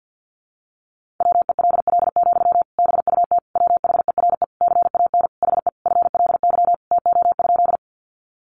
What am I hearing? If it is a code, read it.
Morse code, heard as a keyed tone, about 50 wpm, 700 Hz. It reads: WELL, BUT WHERE CAN HE FLY TO?
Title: Sentences from Top 500 Words 50wpm